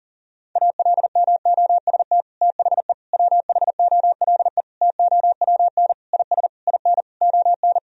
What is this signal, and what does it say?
Morse code, heard as a keyed tone, about 40 wpm, 700 Hz. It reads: ALMOST THE WHOLE TOWN IS IN ON
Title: Sentences from Top 1000 Words - Review Part 4 - 40wpm